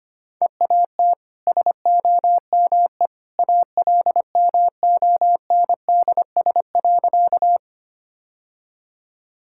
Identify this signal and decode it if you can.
Morse code, heard as a keyed tone, about 25 wpm, 690 Hz. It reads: EAT SOME ALMONDS.
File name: Easy Sentences with Hints - 25wpm